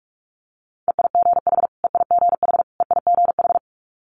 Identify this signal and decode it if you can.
Morse code, heard as a keyed tone, about 46 wpm, 700 Hz. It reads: EIGH EIGH EIGH
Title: Multi-letter Phonemes Repeated 3x 45wpm